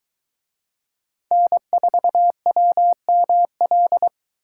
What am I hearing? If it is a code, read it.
Morse code, heard as a keyed tone, about 23 wpm, 700 Hz. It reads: N4WML